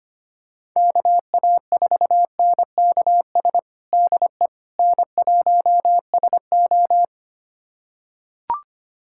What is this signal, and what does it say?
Morse code, heard as a keyed tone, about 25 wpm, 700 Hz. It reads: KA4NKS DE N1SO
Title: USA Callsign DE Callsign 25wpm